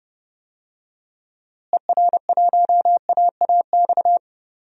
Morse code, heard as a keyed tone, about 30 wpm, 700 Hz. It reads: ER1AAX
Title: DXCC-100 Call Signs - 6 Characters - Repeated 2x - 30wpm